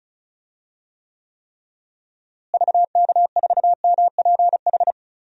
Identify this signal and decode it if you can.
Morse code, heard as a keyed tone, about 34 wpm, 700 Hz. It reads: VK4MPH
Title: DXCC-100 Call Signs - 6 Characters - Repeated 3x - 35wpm